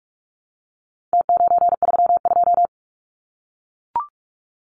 Morse code, heard as a keyed tone, about 44 wpm, 700 Hz. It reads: T932
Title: Numbers - Contest Style - 4-Digit 45wpm